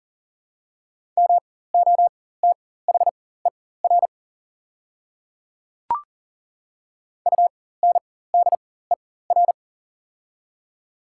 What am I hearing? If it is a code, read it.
Morse code, heard as a keyed tone, about 41 wpm, spaced as for 19 wpm, 700 Hz. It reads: MOTHER UNDER
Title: 4x Character Spacing - Top 300 Words - 40wpm